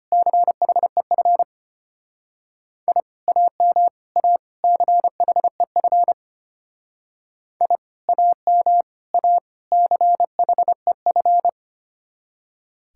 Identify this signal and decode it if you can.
Morse code, heard as a keyed tone, about 33 wpm, 700 Hz. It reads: CHEF I AM A CHEF I AM A CHEF